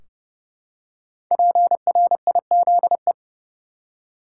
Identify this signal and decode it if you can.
Morse code, heard as a keyed tone, about 30 wpm, 700 Hz. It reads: PRIZE
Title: Five Letter Words - 30wpm